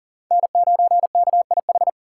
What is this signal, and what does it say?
Morse code, heard as a keyed tone, about 40 wpm, 700 Hz. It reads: N9KIH
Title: USA Call Signs - 3x Character Spacing - 40wpm